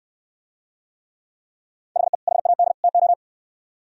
Morse code, heard as a keyed tone, about 66 wpm, 700 Hz. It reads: HE HAD TO